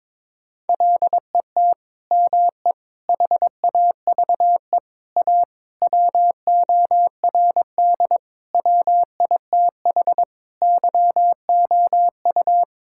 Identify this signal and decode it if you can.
Morse code, heard as a keyed tone, about 21 wpm, 700 Hz. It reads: LET ME HAVE A WORD WITH YOU
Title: Sentences from Top 1000 Words - Review Part 6 - 22wpm